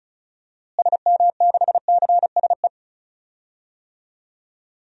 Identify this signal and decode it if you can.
Morse code, heard as a keyed tone, about 34 wpm, 700 Hz. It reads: SM6CSE
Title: DXCC-100 Call Signs - 6 Characters - Repeated 2x - 35wpm